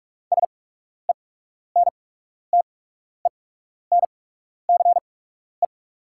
Morse code, heard as a keyed tone, about 43 wpm, spaced as for 12 wpm, 690 Hz. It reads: SENTENCE
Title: 8x Character Spacing - Top 200 Words - 45wpm